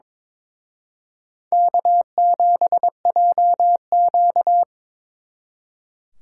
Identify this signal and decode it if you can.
Morse code, heard as a keyed tone, about 25 wpm, 700 Hz. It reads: K7JQ
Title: Speed Racing - USA 1x2 Call Signs 22wpm